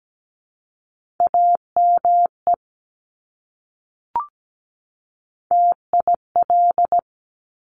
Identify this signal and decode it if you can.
Morse code, heard as a keyed tone, about 17 wpm, 700 Hz. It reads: AME TIL
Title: Top Three Letter Combinations in Words 17wpm